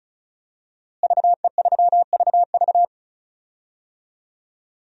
Morse code, heard as a keyed tone, about 36 wpm, 700 Hz. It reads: VE3VV
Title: DXCC-100 Call Signs - 5 Characters - Repeated 3x - 35wpm